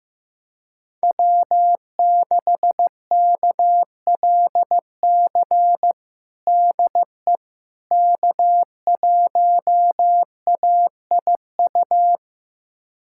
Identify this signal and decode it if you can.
Morse code, heard as a keyed tone, about 15 wpm, 700 Hz. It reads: W6KLC DE K1AIU
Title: USA Callsign DE Callsign - 5 Characters - 15wpm